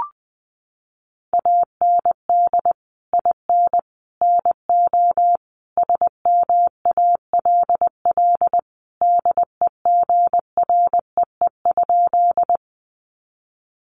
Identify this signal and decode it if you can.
Morse code, heard as a keyed tone, about 20 wpm, 700 Hz. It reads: AND IN NO SMALL DEGREE?
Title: Sentences from Top 1500 Words 20wpm